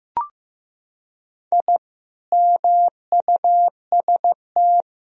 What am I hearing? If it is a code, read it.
Morse code, heard as a keyed tone, about 15 wpm, 700 Hz. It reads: I MUST
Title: Sentences from Top 1000 Words - Review Part 2 - 15wpm